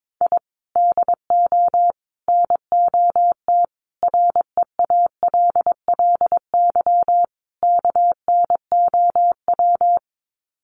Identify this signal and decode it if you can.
Morse code, heard as a keyed tone, about 22 wpm, 700 Hz. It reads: I DO NOT REALLY KNOW
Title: Sets of 5 Words Encore 22wpm